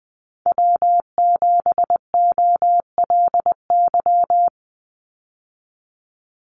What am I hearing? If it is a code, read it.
Morse code, heard as a keyed tone, about 20 wpm, 700 Hz. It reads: W7OLY
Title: Call Signs Repeated 3x - 20wpm